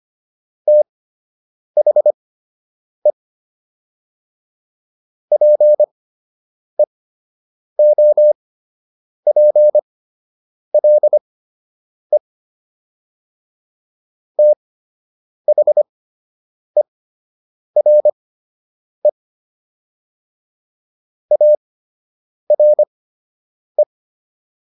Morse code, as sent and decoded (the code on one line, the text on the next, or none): - .... / .--. --- .--. .-.. / - .... .-. / .- .-. .
THE PEOPLE THERE ARE